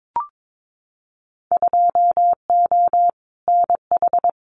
Morse code, heard as a keyed tone, about 22 wpm, 700 Hz. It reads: 2O NH
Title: Winter Field Day Exchanges 22wpm